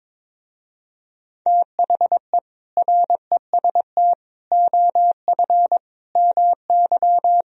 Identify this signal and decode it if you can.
Morse code, heard as a keyed tone, about 23 wpm, 700 Hz. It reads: THE REST OF MY
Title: Speed-Racing - Sets of 5 Words 22wpm